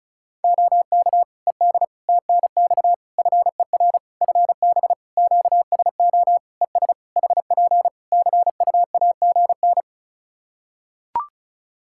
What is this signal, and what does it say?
Morse code, heard as a keyed tone, about 34 wpm, 700 Hz. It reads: OK ED TNX FER FB QSO ES HP CUAGN
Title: Practice QSO - Exchange 4 - Ending - 35wpm